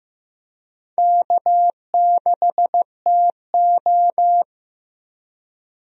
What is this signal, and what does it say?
Morse code, heard as a keyed tone, about 15 wpm, 700 Hz. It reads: K6TO
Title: USA Call Signs - 4 Characters - Repeated 2x - 15wpm